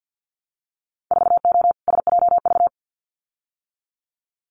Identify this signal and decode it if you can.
Morse code, heard as a keyed tone, about 50 wpm, 700 Hz. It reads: 4O SJV
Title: Winter Field Day Exchanges Encore 50wpm